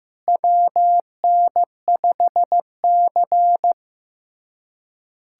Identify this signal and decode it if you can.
Morse code, heard as a keyed tone, about 16 wpm, 700 Hz. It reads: WN5C
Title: USA Callsign DE Callsign - 4 Characters - 15wpm